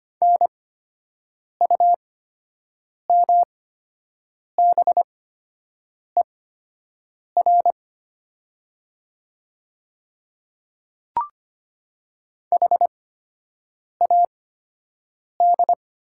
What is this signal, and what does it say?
Morse code, heard as a keyed tone, about 25 wpm, spaced as for 7 wpm, 700 Hz. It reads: NUMBER HAD
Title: 8x Character Spacing - Top 100 Words - 25wpm